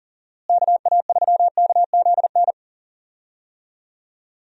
Morse code, heard as a keyed tone, about 40 wpm, 700 Hz. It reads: KA3KZN
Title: USA Call Signs - 6 Characters - Repeated 3x - 40wpm